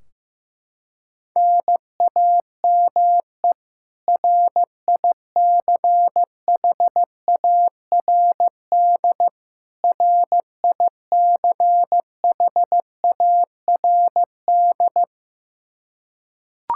Morse code, heard as a keyed tone, about 16 wpm, 700 Hz. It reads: NAME RICHARD RICHARD
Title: Practice QSO - Exchange 1 - Beginning - 15wpm